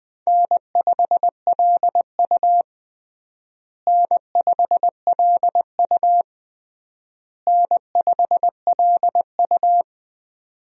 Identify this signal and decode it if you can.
Morse code, heard as a keyed tone, about 20 wpm, 700 Hz. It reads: N5LU N5LU N5LU
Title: USA Call Signs - 4 Characters - Repeated 3x - 20wpm